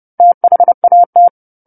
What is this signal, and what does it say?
Morse code, heard as a keyed tone, about 30 wpm, 700 Hz. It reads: THAT
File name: 10-Chapter-10 30 WPM